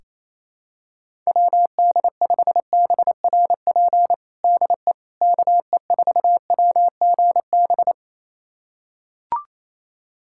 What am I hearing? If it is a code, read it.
Morse code, heard as a keyed tone, about 28 wpm, 700 Hz. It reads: WD5BRP DE KE4WGB
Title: USA Callsign DE Callsign 28wpm